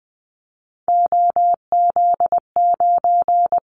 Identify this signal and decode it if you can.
Morse code, heard as a keyed tone, about 20 wpm, 700 Hz. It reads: OZ9
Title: Callsign Prefixes for Top 100 DXCC - 20wpm